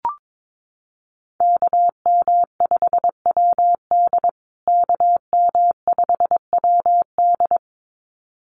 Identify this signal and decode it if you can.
Morse code, heard as a keyed tone, about 22 wpm, 700 Hz. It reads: KM5WD KM5WD
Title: USA Call Signs Repeated 2x 22wpm